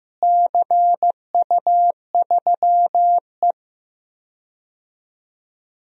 Morse code, heard as a keyed tone, about 15 wpm, 700 Hz. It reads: CU3E
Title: DXCC-100 Call Signs - 4 Characters - Repeated 3x - 15wpm